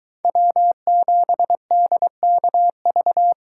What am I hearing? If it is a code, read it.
Morse code, heard as a keyed tone, about 22 wpm, 700 Hz. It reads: W7DKV